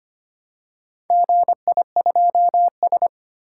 Morse code, heard as a keyed tone, about 25 wpm, 700 Hz. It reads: GI2S
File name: practice / 25wpm / Four Letter-Number Combinations - 25wpm